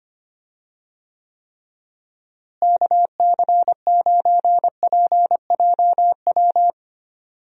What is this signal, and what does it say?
Morse code, heard as a keyed tone, about 25 wpm, 700 Hz. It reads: KC9PJW
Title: USA Call Signs - 6 Characters - Repeated 3x - 25wpm